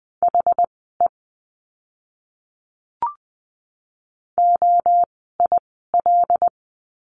Morse code, heard as a keyed tone, about 20 wpm, spaced as for 15 wpm, 700 Hz. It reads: HE OIL
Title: 094 Lesson 16f - Letter c - Words 20wpm